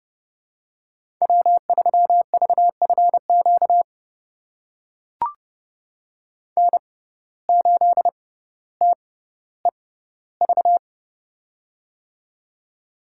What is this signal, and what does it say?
Morse code, heard as a keyed tone, about 30 wpm, spaced as for 15 wpm, 700 Hz. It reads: W3VFQ N8TEV
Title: USA Call Signs - 5 Characters - 6x Character Spacing - 30wpm